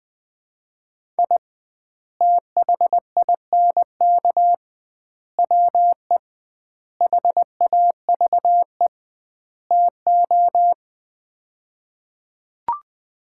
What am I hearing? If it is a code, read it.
Morse code, heard as a keyed tone, about 20 wpm, 700 Hz. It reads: I THINK WE HAVE TO
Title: 2x Word-Spacing - Sets of 5 Words - 20wpm